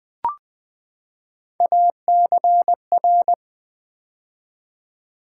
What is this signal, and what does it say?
Morse code, heard as a keyed tone, about 20 wpm, 700 Hz. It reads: ACR
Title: Build A Word - 3 Letters At A Time - 6-Letter Words - 20wpm